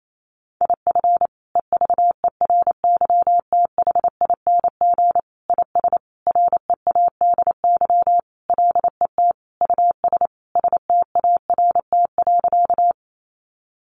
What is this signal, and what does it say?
Morse code, heard as a keyed tone, about 28 wpm, 700 Hz. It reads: IF EVERYTHING IS READY LET US START.
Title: Sentences from Top 700 Words 28wpm